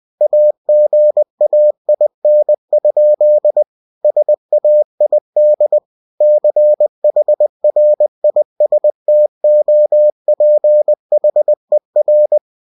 Morse code, heard as a keyed tone, about 20 wpm, 600 Hz. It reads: AGAIN? SAID CHRISTOPHER
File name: cw-books / winnie-the-pooh / winnie-20wpm-page-0018